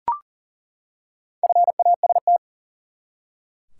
Morse code, heard as a keyed tone, about 40 wpm, 700 Hz. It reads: FAST
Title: Four Letter Words - 40wpm